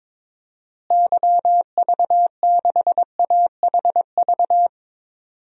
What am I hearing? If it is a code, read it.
Morse code, heard as a keyed tone, about 22 wpm, 700 Hz. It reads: YV6AHV
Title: DXCC-100 Call Signs - 6 Characters - Repeated 2x - 22wpm